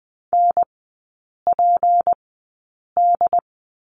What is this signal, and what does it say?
Morse code, heard as a keyed tone, about 20 wpm, spaced as for 8 wpm, 690 Hz. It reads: NPD
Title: 191 Lesson 31e - Letter v - Three Characters 20wpm